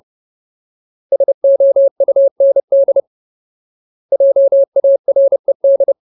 Morse code, heard as a keyed tone, about 30 wpm, 550 Hz. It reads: SOUND JARED
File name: Ultimate Word List / 30WPM Ultimate